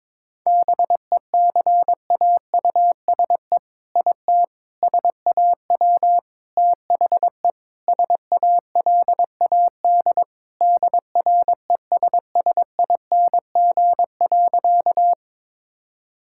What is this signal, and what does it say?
Morse code, heard as a keyed tone, about 22 wpm, 700 Hz. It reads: BECAUSE IT SAW THE SALAD DRESSING.
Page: morse-code-ninja-practice-22wpm